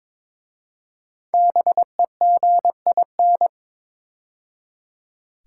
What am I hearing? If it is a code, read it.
Morse code, heard as a keyed tone, about 23 wpm, 690 Hz. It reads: BEGIN
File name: 5 Letter Words - First 3 Letters Given - 22wpm